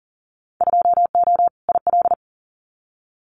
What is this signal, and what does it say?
Morse code, heard as a keyed tone, about 41 wpm, 700 Hz. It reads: 2O IL
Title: Winter Field Day Exchanges 40wpm